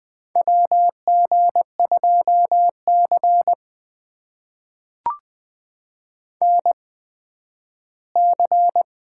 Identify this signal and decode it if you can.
Morse code, heard as a keyed tone, about 20 wpm, spaced as for 12 wpm, 700 Hz. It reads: WG2C NC
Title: USA Call Signs - 4 Characters - 8x Character Spacing - 20wpm